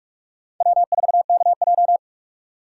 Morse code, heard as a keyed tone, about 44 wpm, 700 Hz. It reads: W4KJ